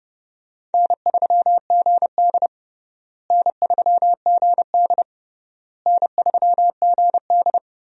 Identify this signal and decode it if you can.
Morse code, heard as a keyed tone, about 30 wpm, 700 Hz. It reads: N3GD N3GD N3GD
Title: USA Call Signs - 4 Characters - Repeated 3x - 30wpm